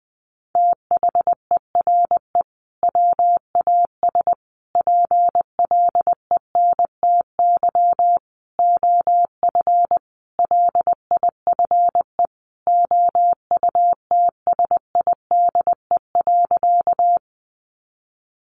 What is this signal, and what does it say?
Morse code, heard as a keyed tone, about 20 wpm, 700 Hz. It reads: THERE WAS PLENTY OF LIFE OUTSIDE.
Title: Sentences from Top 1300 Words 20wpm